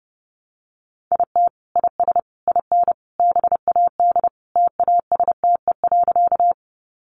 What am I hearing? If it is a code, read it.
Morse code, heard as a keyed tone, about 30 wpm, 700 Hz. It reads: IT IS IN BAD TASTE.
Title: Sentences from Top 1100 Words 30wpm